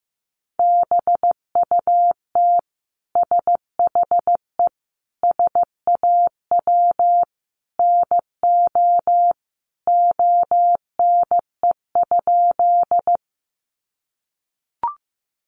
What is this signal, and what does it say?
Morse code, heard as a keyed tone, about 14 wpm, 700 Hz. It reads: BUT SHE SAW NO ONE?
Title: Sentences from Top 300 Words 15wpm